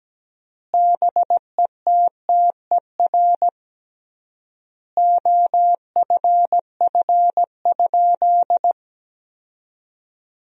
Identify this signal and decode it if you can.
Morse code, heard as a keyed tone, about 17 wpm, 700 Hz. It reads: BETTER OFF?
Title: 3x Word-Spacing - Sentences from Top 500 Words - 17wpm